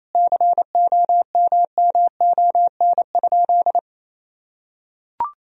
Morse code, heard as a keyed tone, about 28 wpm, 700 Hz. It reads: COMMON?
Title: Sentences from Top 1000 Words - Review Part 3 - 28wpm